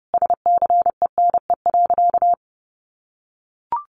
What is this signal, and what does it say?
Morse code, heard as a keyed tone, about 30 wpm, 700 Hz. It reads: SCENE.
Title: Sentences from Top 1300 Words 30wpm